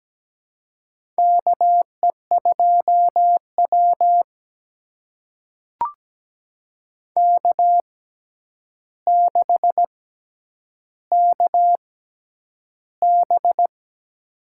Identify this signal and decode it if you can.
Morse code, heard as a keyed tone, about 17 wpm, spaced as for 9 wpm, 700 Hz. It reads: KE2W K6KB
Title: USA Call Signs - 4 Characters - 6x Character Spacing - 17wpm